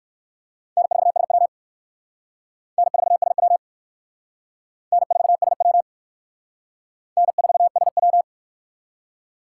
Morse code, heard as a keyed tone, about 57 wpm, 700 Hz. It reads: N4SW N4SW N4SW N4SW